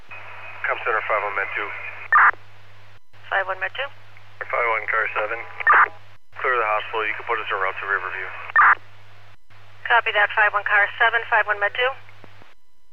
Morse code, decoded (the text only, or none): none